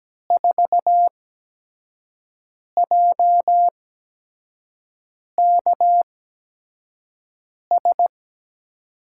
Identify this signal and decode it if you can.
Morse code, heard as a keyed tone, about 17 wpm, spaced as for 5 wpm, 700 Hz. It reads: 4JKS
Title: USA Call Signs - 5 Characters - 8x Character Spacing - 17wpm